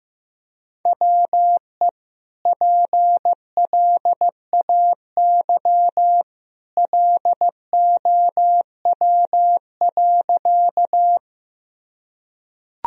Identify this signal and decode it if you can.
Morse code, heard as a keyed tone, about 15 wpm, 700 Hz. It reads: WE PLAY LOW.
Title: Sentences from Top 1000 Words - Review Part 1 - 15wpm